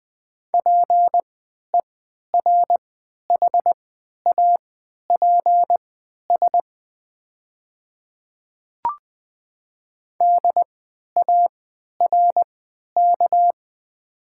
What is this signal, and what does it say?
Morse code, heard as a keyed tone, about 20 wpm, spaced as for 11 wpm, 700 Hz. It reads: PERHAPS DARK
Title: 3x Character Spacing - Top 500 Words - 20wpm